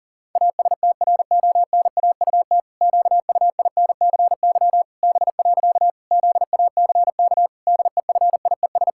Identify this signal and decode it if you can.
Morse code, heard as a keyed tone, about 40 wpm, 700 Hz. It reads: ASTRONAUT QUINCY B. ZACK DEFIES
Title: Pangram Challenge v2 - 40wpm